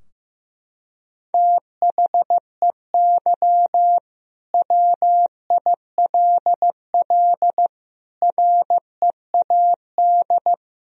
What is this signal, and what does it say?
Morse code, heard as a keyed tone, about 15 wpm, 700 Hz. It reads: THEY WILL READ
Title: Sentences from Top 1000 Words - Review Part 4 - 15wpm